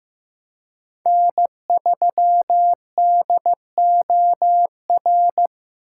Morse code, heard as a keyed tone, about 15 wpm, 700 Hz. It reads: N3DOR
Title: USA Call Signs - 5 Characters - 4x Character Spacing - 15wpm